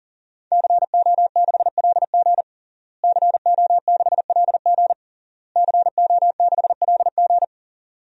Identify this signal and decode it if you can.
Morse code, heard as a keyed tone, about 40 wpm, 700 Hz. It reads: CO6LG CO6LG CO6LG
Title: DXCC-100 Call Signs - 5 Characters - Repeated 3x - 40wpm